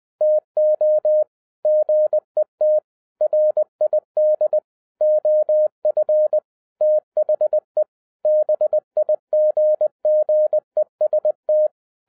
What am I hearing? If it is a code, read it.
Morse code, heard as a keyed tone, about 20 wpm, 610 Hz. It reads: TO GET RID OF THE BIGGEST
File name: 2011-07-mp3 / n58 BACK FROM A BIG ACCIDENT 20wpm